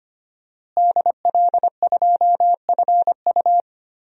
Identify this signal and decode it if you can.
Morse code, heard as a keyed tone, about 25 wpm, 700 Hz. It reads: DL2FU
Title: DXCC-100 Call Signs - 5 Characters - Repeated 2x - 25wpm